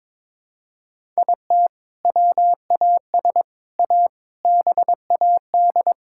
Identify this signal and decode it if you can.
Morse code, heard as a keyed tone, about 22 wpm, 700 Hz. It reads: IT WAS A BAD
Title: Sentences from Top 1000 Words - Review Part 5 - 22wpm